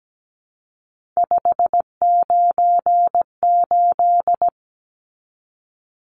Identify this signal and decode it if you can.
Morse code, heard as a keyed tone, about 17 wpm, 700 Hz. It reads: 598
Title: Numbers - 3-Digit 17wpm